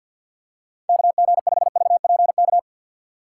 Morse code, heard as a keyed tone, about 53 wpm, 700 Hz. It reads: KG5VPK
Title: USA Call Signs - 2x Character Spacing - 50wpm